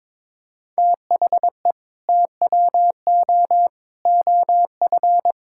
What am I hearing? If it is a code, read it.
Morse code, heard as a keyed tone, about 22 wpm, 700 Hz. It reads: THE TWO OF
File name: practice / 22wpm / Sentences with 4-Word Phrases - 22wpm